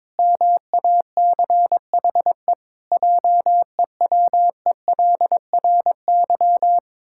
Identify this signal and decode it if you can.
Morse code, heard as a keyed tone, about 22 wpm, 700 Hz. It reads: MACHE JEWELRY
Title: Pangram Challenge v2 - 22wpm